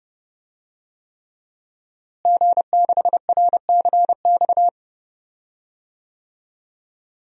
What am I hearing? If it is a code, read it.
Morse code, heard as a keyed tone, about 30 wpm, 700 Hz. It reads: G6RCX
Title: DXCC-100 Call Signs - 5 Characters - Repeated 3x - 30wpm